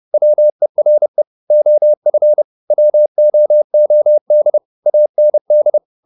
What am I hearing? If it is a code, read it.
Morse code, heard as a keyed tone, about 30 wpm, 600 Hz. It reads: WERE OF WOOD AND
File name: the old man and the sea 2 30wpm